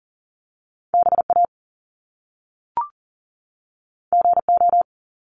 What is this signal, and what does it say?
Morse code, heard as a keyed tone, about 42 wpm, 700 Hz. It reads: BA GO
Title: Top Two Letter Combinations in Words 40wpm